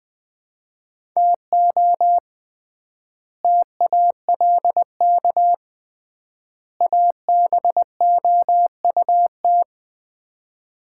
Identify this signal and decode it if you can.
Morse code, heard as a keyed tone, about 20 wpm, 700 Hz. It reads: TO TALK ABOUT
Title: 3x Word-Spacing - Sets of 3 Words - 20wpm